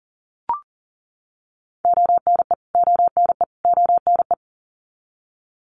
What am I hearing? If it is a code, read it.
Morse code, heard as a keyed tone, about 40 wpm, 700 Hz. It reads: ONE ONE ONE
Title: Top 200 Words Repeated 40wpm